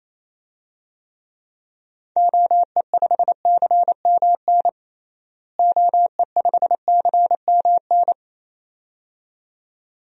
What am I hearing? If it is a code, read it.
Morse code, heard as a keyed tone, about 28 wpm, 700 Hz. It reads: OE5CMN OE5CMN
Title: DXCC-100 Call Signs - 6 Characters - Repeated 2x - 28wpm